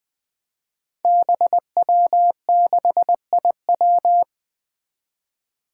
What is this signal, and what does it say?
Morse code, heard as a keyed tone, about 20 wpm, 700 Hz. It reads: BW6IW
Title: Five Challenging Letter-Number Combinations 20wpm